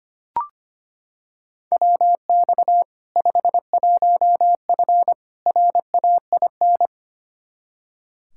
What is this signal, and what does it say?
Morse code, heard as a keyed tone, about 25 wpm, 700 Hz. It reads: WX 51F RAIN
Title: K1USN SST QSO Practice 25wpm